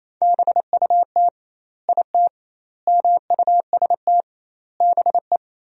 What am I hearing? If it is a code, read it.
Morse code, heard as a keyed tone, about 28 wpm, 700 Hz. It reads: BUT IT MUST BE
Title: 2x Word-Spacing - Sentences from Top 500 Words - 28wpm